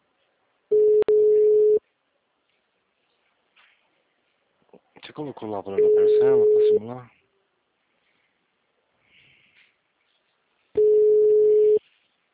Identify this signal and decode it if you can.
Ringback tone (Europe, 425 Hz), with speech.